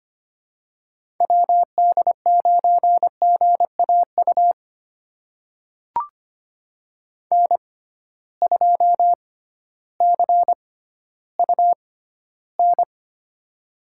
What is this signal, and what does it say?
Morse code, heard as a keyed tone, about 25 wpm, spaced as for 13 wpm, 700 Hz. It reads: WD9GAU N2CUN
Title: USA Call Signs - 6x Character Spacing - 25wpm